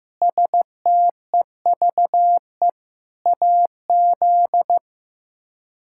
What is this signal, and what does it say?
Morse code, heard as a keyed tone, about 15 wpm, 700 Hz. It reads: STEVE AZ
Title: K1USN SST QSO Practice - 15wpm